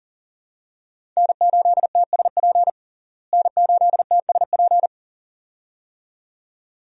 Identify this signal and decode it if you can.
Morse code, heard as a keyed tone, about 40 wpm, 700 Hz. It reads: N8TSP N8TSP
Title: USA Call Signs - 5 Characters - Repeated 2x - 40wpm